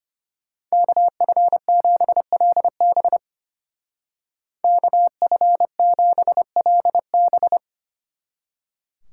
Morse code, heard as a keyed tone, about 30 wpm, 700 Hz. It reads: KF7LB KF7LB
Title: Speed Racing - USA 2x2 Call Signs 25wpm